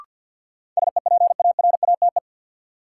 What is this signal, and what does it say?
Morse code, heard as a keyed tone, about 53 wpm, 700 Hz. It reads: SEPARATE
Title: Sentences from Top 1000 Words - Review Part 7 - New Word Drill - 50wpm